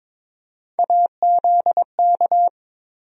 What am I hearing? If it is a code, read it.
Morse code, heard as a keyed tone, about 22 wpm, 700 Hz. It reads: AZK
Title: USA Call Sign Postfixes - 3 Letters 22wpm